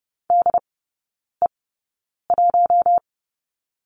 Morse code, heard as a keyed tone, about 30 wpm, spaced as for 9 wpm, 700 Hz. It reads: DE1
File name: Lesson 17e - Letter u - Three Characters 30wpm